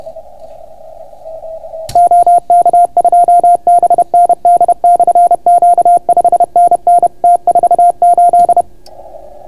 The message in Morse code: OK2BND/Q 5NN T48

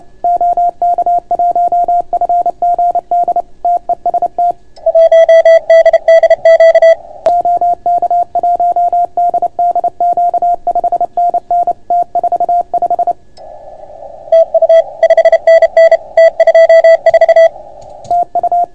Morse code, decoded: OK1FGD TEST 1DDQ OK1DDQ 5NN T45 TU 5NN T24 TU

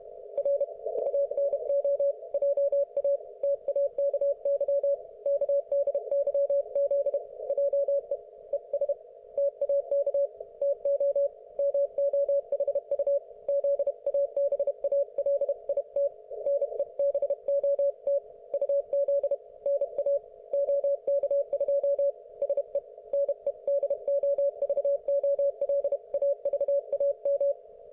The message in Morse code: R ANO JA TAKY KDYZ JE ES TAK TO MOHU ZABALIT NEBOT UZ NA OK2 SE NEDOVOLAVAM

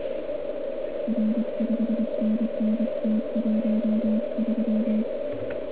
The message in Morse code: R 5NN T13

